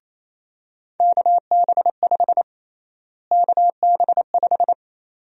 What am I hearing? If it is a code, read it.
Morse code, heard as a keyed tone, about 28 wpm, 700 Hz. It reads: KB5 KB5
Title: USA Call Sign Prefixes - Repeated 2x - 28wpm